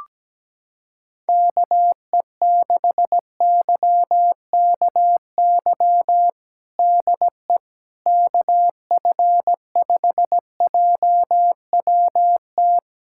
Morse code, heard as a keyed tone, about 17 wpm, 700 Hz. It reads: KE6YKY DE KF5JWT
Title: USA Callsign DE Callsign - 6 Characters - 17wpm